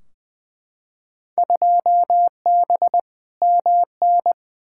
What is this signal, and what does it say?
Morse code, heard as a keyed tone, about 21 wpm, 700 Hz. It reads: 2B MN
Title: Field Day Exchanges - 20wpm